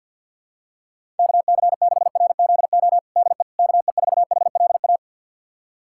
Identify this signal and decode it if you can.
Morse code, heard as a keyed tone, about 53 wpm, 690 Hz. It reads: KC6RZO DE KE4HLA
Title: USA Callsign DE Callsign - 6 Characters - 50wpm